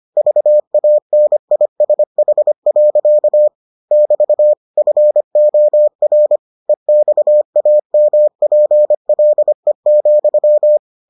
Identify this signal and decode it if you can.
Morse code, heard as a keyed tone, about 25 wpm, 600 Hz. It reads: VANISH. <BT> FOR EXAMPLE,